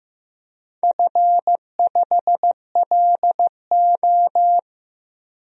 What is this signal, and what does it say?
Morse code, heard as a keyed tone, about 15 wpm, 700 Hz. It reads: F5LO
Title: DXCC-100 Call Signs - 4 Characters - 15wpm